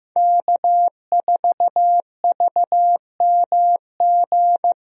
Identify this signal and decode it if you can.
Morse code, heard as a keyed tone, about 15 wpm, 700 Hz. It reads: K4VMG
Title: USA Call Signs - 5 Characters - Repeated 2x - 15wpm